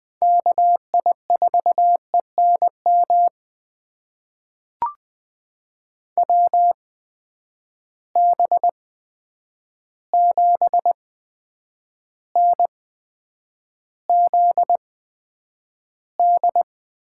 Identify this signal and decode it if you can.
Morse code, heard as a keyed tone, about 20 wpm, spaced as for 9 wpm, 700 Hz. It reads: KI4ENM WB7NZD
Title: USA Call Signs - 6 Characters - 8x Character Spacing - 20wpm